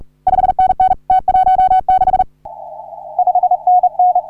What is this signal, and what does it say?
Morse code, heard as a keyed tone, about 38 wpm, 750 Hz. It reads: 5NN T16 5NN
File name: CW / HA3OU RHOBCW REC